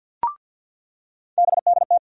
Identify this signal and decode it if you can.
Morse code, heard as a keyed tone, about 53 wpm, 700 Hz. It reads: BDT
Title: USA Call Sign Postfixes - 3 Letters 50wpm